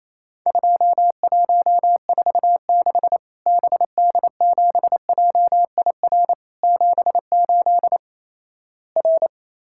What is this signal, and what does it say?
Morse code, heard as a keyed tone, about 28 wpm, 700 Hz. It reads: 2146 BD7JIR 78 R
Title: Worked All Europe DX Contest - QTC Exchange - 28wpm